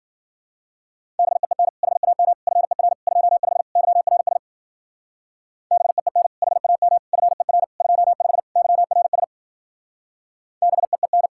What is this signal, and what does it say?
Morse code, heard as a keyed tone, about 50 wpm, 700 Hz. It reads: BEEN HAM FER 25 YRS BEEN HAM FER 25 YRS BEEN